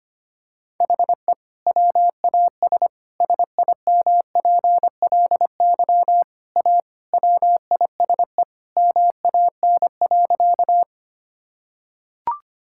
Morse code, heard as a keyed tone, about 25 wpm, 700 Hz. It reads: HE WAS SIMPLY A WISE MAN.
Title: Sentences from Top 1600 Words 25wpm